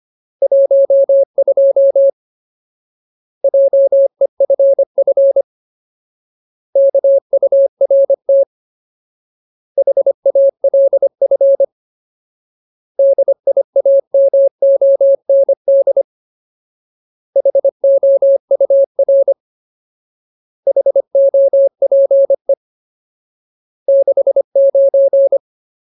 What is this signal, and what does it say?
Morse code, heard as a keyed tone, about 25 wpm, 550 Hz. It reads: 12 JEFF KURT HALF DIAMOND HOUR HOPE 69
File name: Ultimate Word List / 25WPM Ultimate